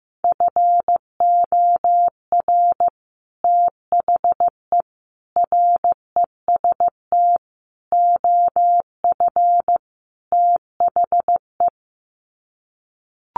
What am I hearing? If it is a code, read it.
Morse code, heard as a keyed tone, about 15 wpm, 700 Hz. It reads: FOR THE REST OF THE
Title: Sets of 5 Words 15wpm